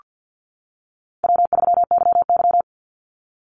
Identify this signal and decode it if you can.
Morse code, heard as a keyed tone, about 48 wpm, 700 Hz. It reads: W3YY
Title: Easy Call Signs - 50wpm